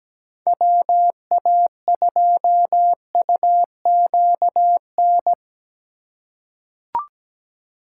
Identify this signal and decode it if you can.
Morse code, heard as a keyed tone, about 17 wpm, 700 Hz. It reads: WA2UQN
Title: USA Call Signs - 6 Characters - 2x Character Spacing - 17wpm